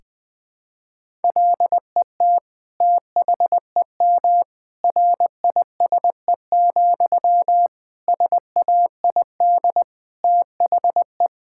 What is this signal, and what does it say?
Morse code, heard as a keyed tone, about 20 wpm, 700 Hz. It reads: LET THEM RISE, SAID THE
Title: Sentences from Top 1000 Words - Review Part 6 - 20wpm